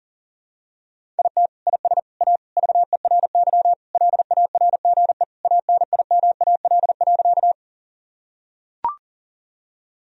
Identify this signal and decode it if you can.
Morse code, heard as a keyed tone, about 40 wpm, 700 Hz. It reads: IT IS A VERY LARGE ANIMAL.
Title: Sentences from Top 1000 Words - Review Part 1 - 40wpm